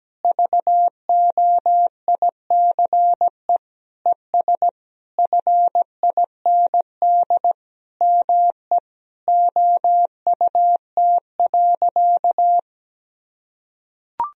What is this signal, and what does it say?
Morse code, heard as a keyed tone, about 17 wpm, 700 Hz. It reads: VOICE ES FIND ME OUT.
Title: Sentences from Top 500 Words and Abbreviations 17wpm